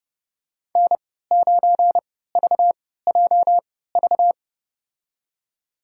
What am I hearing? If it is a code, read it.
Morse code, heard as a keyed tone, about 30 wpm, spaced as for 17 wpm, 700 Hz. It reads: N9VJV
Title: USA Call Signs - 3x Character Spacing - 30wpm